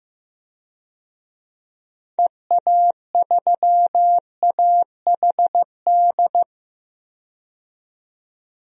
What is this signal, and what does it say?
Morse code, heard as a keyed tone, about 15 wpm, 700 Hz. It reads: EA3AHD
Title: DXCC-100 Call Signs - 6 Characters - 15wpm